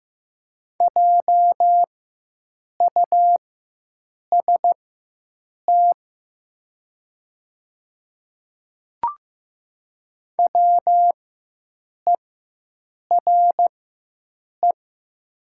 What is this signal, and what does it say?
Morse code, heard as a keyed tone, about 15 wpm, spaced as for 7 wpm, 700 Hz. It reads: JUST WERE
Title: 4x Character Spacing - Top 200 Words - 15wpm